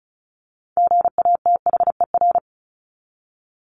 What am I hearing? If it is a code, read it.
Morse code, heard as a keyed tone, about 34 wpm, 700 Hz. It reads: GATHER
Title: Top 1501-1750 Words 35wpm